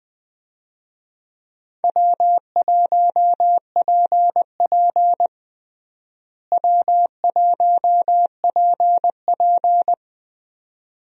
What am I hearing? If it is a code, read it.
Morse code, heard as a keyed tone, about 20 wpm, 700 Hz. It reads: W1PP W1PP
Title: USA Call Signs - 4 Characters - Repeated 2x - 20wpm